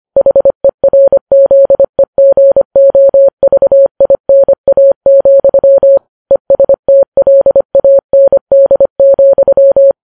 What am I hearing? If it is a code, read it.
Morse code, heard as a keyed tone, about 25 wpm, 550 Hz. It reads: HERZEGOVINA, ESTLAND,
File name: Files Web tls / Morse on9fv / morse25kps